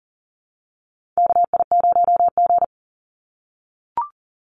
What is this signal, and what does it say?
Morse code, heard as a keyed tone, about 40 wpm, 700 Hz. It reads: KI0G
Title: NA SOTA Chaser Callsigns 40wpm